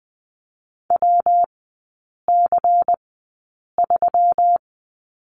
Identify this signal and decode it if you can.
Morse code, heard as a keyed tone, about 20 wpm, spaced as for 8 wpm, 700 Hz. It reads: WC3
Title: 145 Lesson 24e - Letter y - Three Characters 20wpm